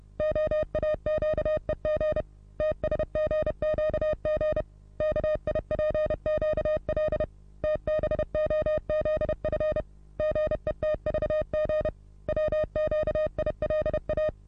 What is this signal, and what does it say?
Morse code, heard as a keyed tone, about 30 wpm, 620 Hz. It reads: OAQEG TSGQG KIPQL TBOZF GETVG WQILA
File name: repos / Morsekurs / DJ4UF Morsen27